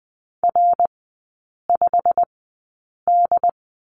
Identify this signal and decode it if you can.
Morse code, heard as a keyed tone, about 20 wpm, spaced as for 8 wpm, 700 Hz. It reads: R5D
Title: 093 Lesson 16e - Letter c - Three Characters 20wpm